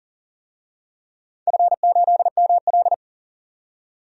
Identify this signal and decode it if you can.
Morse code, heard as a keyed tone, about 40 wpm, 700 Hz. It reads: F8ML
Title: DXCC-100 Call Signs - 4 Characters - 40wpm